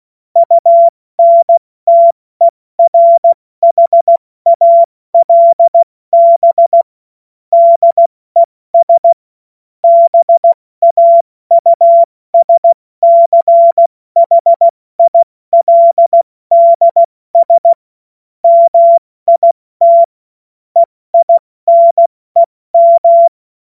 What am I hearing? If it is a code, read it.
Morse code, heard as a keyed tone, about 16 wpm, 680 Hz. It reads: UNTERHALB DES BAUSCHILDS MIT EINEM
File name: Morsekurs / Witze / Witz 23. Dezember 2025 70 F80